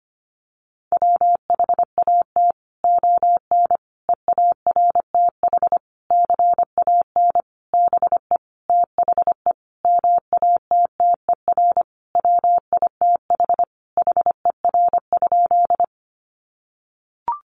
Morse code, heard as a keyed tone, about 25 wpm, 700 Hz. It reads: WHAT ON EARTH CAN BE THE MATTER WITH HER?
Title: Sentences from Top 700 Words 25wpm